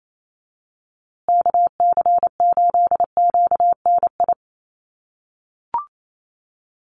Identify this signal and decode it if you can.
Morse code, heard as a keyed tone, about 28 wpm, 700 Hz. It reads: KC8QNI
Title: USA Call Signs Repeated 2x 28wpm